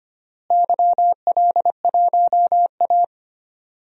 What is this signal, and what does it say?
Morse code, heard as a keyed tone, about 25 wpm, 700 Hz. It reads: YL1A
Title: DXCC-100 Call Signs - 4 Characters - Repeated 3x - 25wpm